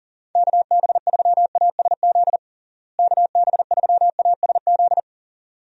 Morse code, heard as a keyed tone, about 37 wpm, 700 Hz. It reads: KB3ASZ KB3ASZ
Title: USA Call Signs - 6 Characters - Repeated 2x - 40wpm